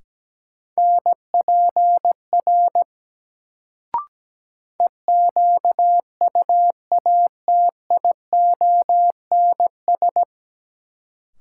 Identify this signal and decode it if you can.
Morse code, heard as a keyed tone, about 17 wpm, 700 Hz. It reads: NPR EQUATIONS
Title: Sets of 5 Words - New Word Drill - 17wpm